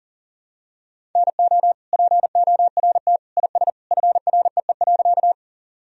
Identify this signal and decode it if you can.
Morse code, heard as a keyed tone, about 40 wpm, 700 Hz. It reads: NO PORT IS FREE.
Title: Sentences from Top 1000 Words - Review Part 1 - 40wpm